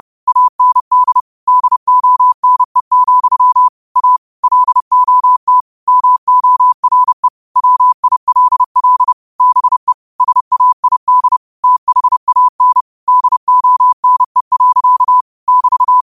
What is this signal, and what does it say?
Morse code, heard as a keyed tone, about 30 wpm, 1000 Hz. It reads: AND DONE, A LOT MORE WILL BE SAID THAN DONE. <BT>
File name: qotdfiles / 30wpm